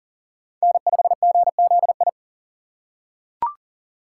Morse code, heard as a keyed tone, about 40 wpm, 700 Hz. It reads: N5GZI
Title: USA Call Signs - 5 Characters - 4x Character Spacing - 40wpm